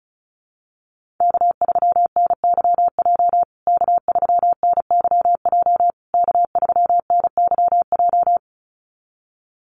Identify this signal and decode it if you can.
Morse code, heard as a keyed tone, about 34 wpm, 700 Hz. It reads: K3NYJ K3NYJ K3NYJ
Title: USA Call Signs Repeated 3x 35wpm